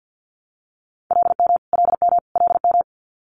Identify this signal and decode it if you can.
Morse code, heard as a keyed tone, about 50 wpm, 690 Hz. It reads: LM LM LM
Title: Multi-letter Phonemes Repeated 3x 50wpm